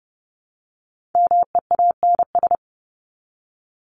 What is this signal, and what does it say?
Morse code, heard as a keyed tone, about 30 wpm, 700 Hz. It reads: MEANS
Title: ES:S Suffix - Top 100 Words with Base Words 30wpm